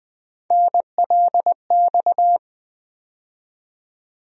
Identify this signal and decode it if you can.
Morse code, heard as a keyed tone, about 20 wpm, 700 Hz. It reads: NLX
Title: USA Call Sign Postfixes - 3 Letters 20wpm